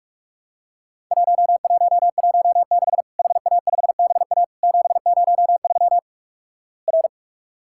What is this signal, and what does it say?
Morse code, heard as a keyed tone, about 48 wpm, 700 Hz. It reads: 1116 HA5BA 703 R
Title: Worked All Europe DX Contest - QTC Exchange - 45wpm